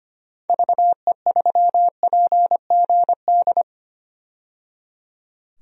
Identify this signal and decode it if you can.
Morse code, heard as a keyed tone, about 25 wpm, 700 Hz. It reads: VE3PGD
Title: DXCC-100 Call Signs - 6 Characters - 25wpm